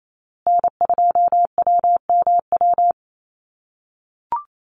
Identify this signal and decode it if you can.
Morse code, heard as a keyed tone, about 28 wpm, 700 Hz. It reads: N2WMW
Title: USA Call Signs Repeated 2x 28wpm